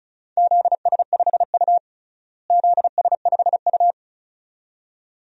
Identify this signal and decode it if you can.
Morse code, heard as a keyed tone, about 36 wpm, 700 Hz. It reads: ZS5U ZS5U
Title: DXCC-100 Call Signs - 4 Characters - Repeated 2x - 35wpm